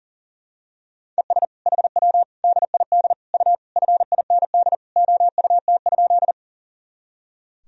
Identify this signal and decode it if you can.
Morse code, heard as a keyed tone, about 40 wpm, 700 Hz. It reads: ES HW DID U FIND OUT?
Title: Sentences from Top 100 Words and Abbreviations 40wpm